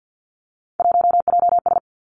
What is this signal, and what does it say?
Morse code, heard as a keyed tone, about 50 wpm, 700 Hz. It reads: 1JS
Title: Three Challenging Letter-Number Combinations 50wpm